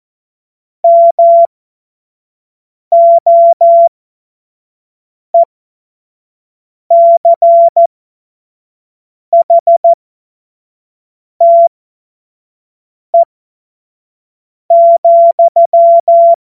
Morse code, heard as a keyed tone, about 14 wpm, spaced as for 5 wpm, 680 Hz. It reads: MOECHTE,
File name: Morsekurs / Witze / Witz 23. Mai 2023 25 F70